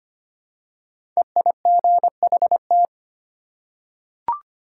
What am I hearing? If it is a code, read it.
Morse code, heard as a keyed tone, about 25 wpm, 700 Hz. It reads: EIGHT
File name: 5 Letter Words - First 3 Letters Given - 25wpm